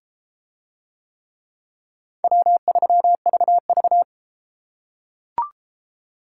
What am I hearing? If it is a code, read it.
Morse code, heard as a keyed tone, about 30 wpm, 700 Hz. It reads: W3VV